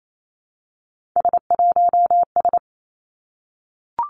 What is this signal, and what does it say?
Morse code, heard as a keyed tone, about 28 wpm, 700 Hz. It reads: S1S
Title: Three Challenging Letter-Number Combinations 28wpm